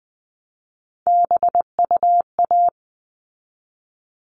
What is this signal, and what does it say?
Morse code, heard as a keyed tone, about 20 wpm, 700 Hz. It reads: BUA
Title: All Cut Numbers - 3-Digit 20wpm